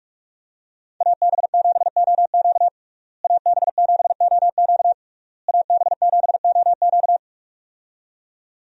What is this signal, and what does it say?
Morse code, heard as a keyed tone, about 46 wpm, 700 Hz. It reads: AB7OQ AB7OQ AB7OQ
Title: USA Call Signs - 5 Characters - Repeated 3x - 45wpm